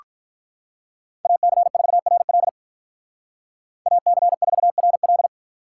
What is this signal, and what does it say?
Morse code, heard as a keyed tone, about 41 wpm, 700 Hz. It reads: AC4RL AC4RL
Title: Speed Racing - USA 2x2 Call Signs 35wpm